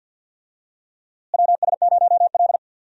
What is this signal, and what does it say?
Morse code, heard as a keyed tone, about 52 wpm, 700 Hz. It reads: WS0L
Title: USA Call Signs - 4 Characters - Repeated 3x - 50wpm